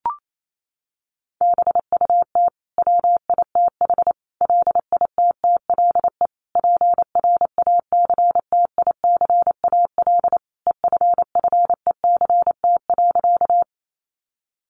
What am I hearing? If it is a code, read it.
Morse code, heard as a keyed tone, about 28 wpm, 700 Hz. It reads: BUT WITH LITTLE PRACTICAL EFFECT.
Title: Sentences from Top 1400 Words 28wpm